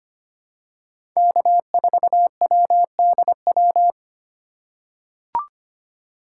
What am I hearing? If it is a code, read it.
Morse code, heard as a keyed tone, about 25 wpm, 700 Hz. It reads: K4WDW
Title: USA Call Signs - 5 Characters - 4x Character Spacing - 25wpm